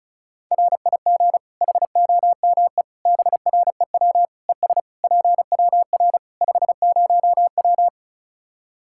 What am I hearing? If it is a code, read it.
Morse code, heard as a keyed tone, about 34 wpm, 700 Hz. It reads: RIG HOME BREW ES PWR 50W
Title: Practice QSO - Exchange 2 - Info - 35wpm